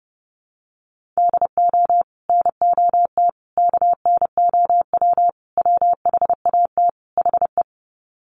Morse code, heard as a keyed tone, about 30 wpm, 700 Hz. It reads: DO NOT KNOW WHAT HE
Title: Sets of 5 Words Encore2 30wpm